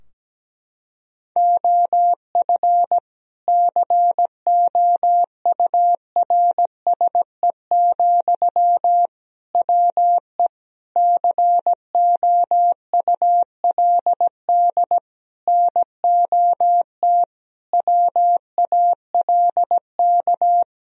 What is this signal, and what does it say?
Morse code, heard as a keyed tone, about 17 wpm, 700 Hz. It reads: OF COURSE, WE COULD NOT WALK
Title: Sentences from Top 1000 Words - Review Part 2 - 17wpm